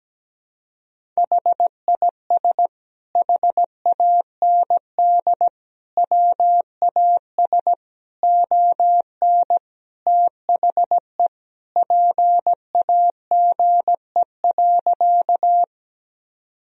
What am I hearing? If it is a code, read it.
Morse code, heard as a keyed tone, about 17 wpm, 700 Hz. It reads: HIS HAND WAS ON THE PAGE.
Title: Sentences from Top 1000 Words - Review Part 1 - 17wpm